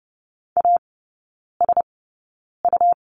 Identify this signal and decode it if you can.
Morse code, heard as a keyed tone, about 30 wpm, spaced as for 9 wpm, 700 Hz. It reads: ASU
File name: Lesson 19e - Letter w - Three Characters 30wpm